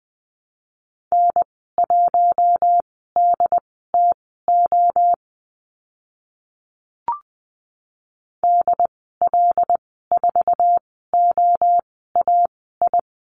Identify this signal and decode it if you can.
Morse code, heard as a keyed tone, about 20 wpm, spaced as for 14 wpm, 700 Hz. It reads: N1DTO DL4OAI
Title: 076 Lesson 13g - Letter L - Call Signs 20wpm